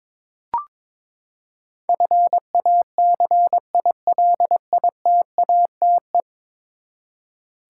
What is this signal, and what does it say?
Morse code, heard as a keyed tone, about 22 wpm, 700 Hz. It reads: FACILITATE
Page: morse-code-ninja-practice-22wpm